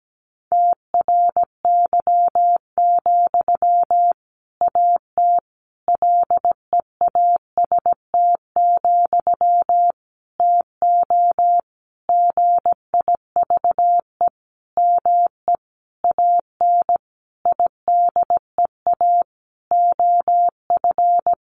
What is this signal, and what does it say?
Morse code, heard as a keyed tone, about 17 wpm, 700 Hz. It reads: TRY, AT LEAST, TO GIVE ME AN IDEA OF
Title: Sentences from Top 700 Words 17wpm